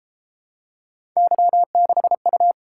Morse code, heard as a keyed tone, about 33 wpm, 700 Hz. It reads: Y6U